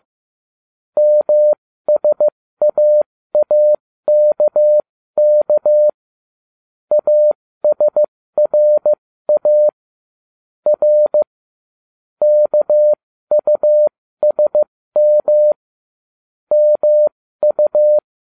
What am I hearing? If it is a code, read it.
Morse code, heard as a keyed tone, about 15 wpm, spaced as for 13 wpm, 600 Hz. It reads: MSAAKK ASRA R KUSM MU